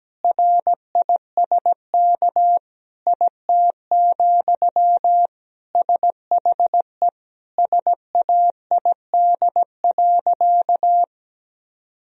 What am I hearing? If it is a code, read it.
Morse code, heard as a keyed tone, about 17 wpm, 700 Hz. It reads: RISK IT, SHE SAID.
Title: Sentences from Top 1000 Words - Review Part 4 - 17wpm